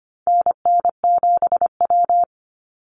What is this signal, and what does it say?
Morse code, heard as a keyed tone, about 25 wpm, 700 Hz. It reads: NN7W
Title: Call Signs Repeated 3x - 25wpm